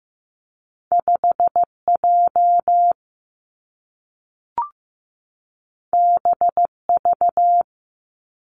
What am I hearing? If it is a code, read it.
Morse code, heard as a keyed tone, about 15 wpm, 700 Hz. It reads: 5J BV
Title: Two Challenging Letter-Number Combinations 15wpm